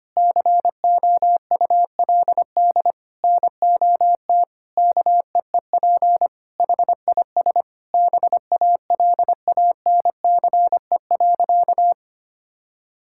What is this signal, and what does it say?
Morse code, heard as a keyed tone, about 25 wpm, 700 Hz. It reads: COULD NOT KEEP HIS BALANCE.
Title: Sentences from Top 1000 Words - Review Part 6 - 25wpm